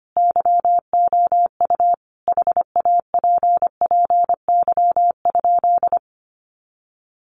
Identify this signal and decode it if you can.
Morse code, heard as a keyed tone, about 25 wpm, 700 Hz. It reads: YOU HAPPY?
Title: Sentences from Top 1000 Words 25wpm